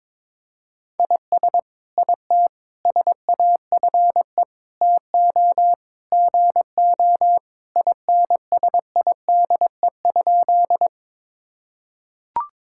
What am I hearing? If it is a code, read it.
Morse code, heard as a keyed tone, about 22 wpm, 700 Hz. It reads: IS IT SAFE TO GO INSIDE?
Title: Sentences from Top 1000 Words - Review Part 6 - 22wpm